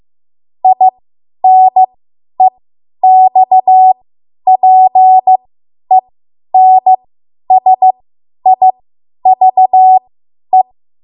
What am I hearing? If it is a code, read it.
Morse code, heard as a keyed tone, about 15 wpm, spaced as for 10 wpm, 750 Hz. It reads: INEXPENSIVE
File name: Morse / 10WPM